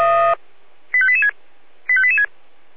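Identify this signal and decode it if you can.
Five-tone selective call (ZVEI), heard as quick runs of five tones, tones of about 70 ms. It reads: 85886 85886
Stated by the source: ZVEI five-tone selective call received with an SDR.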